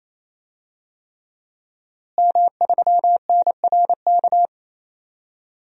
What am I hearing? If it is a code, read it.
Morse code, heard as a keyed tone, about 28 wpm, 690 Hz. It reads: M3NRK
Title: DXCC-100 Call Signs - 5 Characters - Repeated 3x - 28wpm